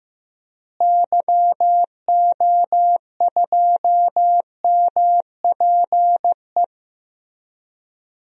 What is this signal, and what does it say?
Morse code, heard as a keyed tone, about 15 wpm, 700 Hz. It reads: YO2MPE
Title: DXCC-100 Call Signs - 6 Characters - Repeated 2x - 15wpm